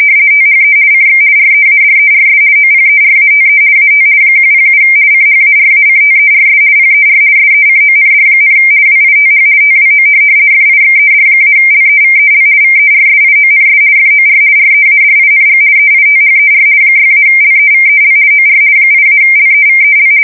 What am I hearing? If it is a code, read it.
RTTY, about 50 baud. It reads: THE CX READERS OPERATE AT 107 CHARACTERS PER SECOND, WITH TWO FIXED-LEVEL VERSIONS FOR READING EITHER 6-LEVEL OR 8-LEVEL
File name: fsk170-50B / teletype-corp